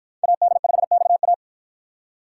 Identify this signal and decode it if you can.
Morse code, heard as a keyed tone, about 53 wpm, 700 Hz. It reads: AD5XA